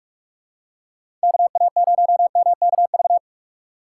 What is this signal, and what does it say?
Morse code, heard as a keyed tone, about 47 wpm, 700 Hz. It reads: KA0MKV
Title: USA Call Signs - 6 Characters - 3x Character Spacing - 45wpm